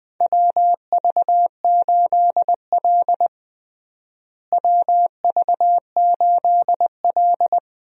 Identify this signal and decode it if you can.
Morse code, heard as a keyed tone, about 21 wpm, 700 Hz. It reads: WV8L WV8L
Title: USA Call Signs - 4 Characters - Repeated 2x - 20wpm